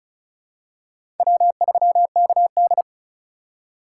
Morse code, heard as a keyed tone, about 36 wpm, 700 Hz. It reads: W3KD
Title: USA Call Signs - 4 Characters - 3x Character Spacing - 35wpm